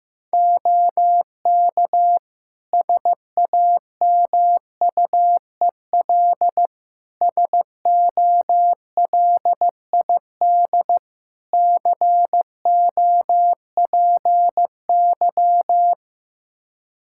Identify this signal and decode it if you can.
Morse code, heard as a keyed tone, about 16 wpm, 700 Hz. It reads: OK SAMUEL SOLID COPY
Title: K1USN SST QSO Practice 15wpm